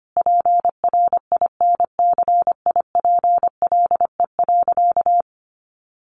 Morse code, heard as a keyed tone, about 25 wpm, 700 Hz. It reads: PRINCIPLE.
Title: Sentences from Top 1400 Words 25wpm